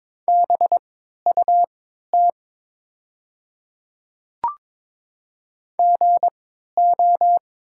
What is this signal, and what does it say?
Morse code, heard as a keyed tone, about 22 wpm, spaced as for 13 wpm, 690 Hz. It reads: BUT GO
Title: 3x Character Spacing - Top 100 Words - 22wpm